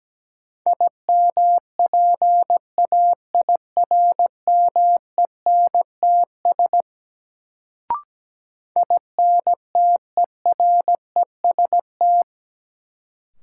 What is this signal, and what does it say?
Morse code, heard as a keyed tone, about 17 wpm, 700 Hz. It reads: IMPAIRMENTS INTEREST
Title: Sets of 5 Words Encore - New Word Drill - 17wpm